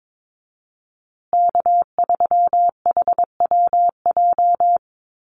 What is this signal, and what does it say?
Morse code, heard as a keyed tone, about 22 wpm, 700 Hz. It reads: K3HWJ
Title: USA Call Signs Repeated 3x 22wpm